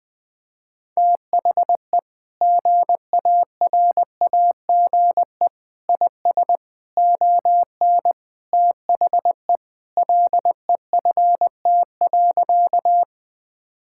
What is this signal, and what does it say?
Morse code, heard as a keyed tone, about 20 wpm, 700 Hz. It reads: THE GARAGE IS ON THE LEFT.
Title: Sentences from Top 1600 Words 20wpm